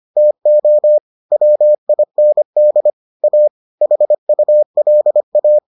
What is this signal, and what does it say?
Morse code, heard as a keyed tone, about 25 wpm, 600 Hz. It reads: TO WIND A HULA